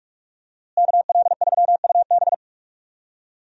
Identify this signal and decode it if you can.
Morse code, heard as a keyed tone, about 45 wpm, 700 Hz. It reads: KL3UB
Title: USA Call Signs - 5 Characters - Repeated 2x - 45wpm